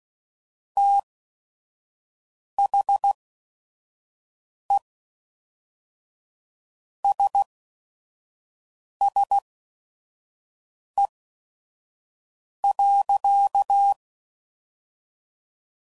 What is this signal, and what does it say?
Morse code, heard as a keyed tone, about 16 wpm, spaced as for 5 wpm, 800 Hz. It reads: THE SSE.
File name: CW Code Groups @5WPM / Weather Sets / Weather 4 14